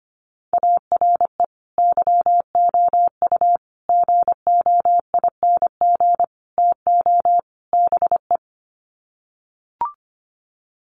Morse code, heard as a keyed tone, about 25 wpm, 700 Hz. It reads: ARE YOU GOING TO BE
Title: Sets of 5 Words Encore 25wpm